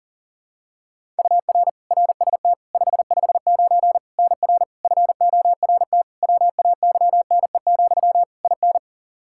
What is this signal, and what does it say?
Morse code, heard as a keyed tone, about 37 wpm, 700 Hz. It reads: UR RST 559 NR FORT WAYNE, IN
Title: Practice QSO - Exchange 1 - Beginning - 40wpm